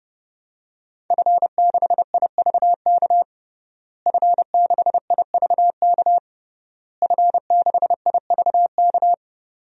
Morse code, heard as a keyed tone, about 32 wpm, 700 Hz. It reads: F6IVK F6IVK F6IVK
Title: DXCC-100 Call Signs - 5 Characters - Repeated 3x - 30wpm